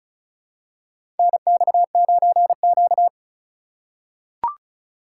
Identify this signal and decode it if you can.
Morse code, heard as a keyed tone, about 36 wpm, 700 Hz. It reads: NX9Q
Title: USA Call Signs - 4 Characters - 4x Character Spacing - 35wpm